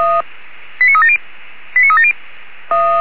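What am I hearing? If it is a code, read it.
Five-tone selective call (ZVEI), heard as quick runs of five tones, tones of about 80 ms. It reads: 89370 89370
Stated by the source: ZVEI five-tone selective call 89366, received by SDR.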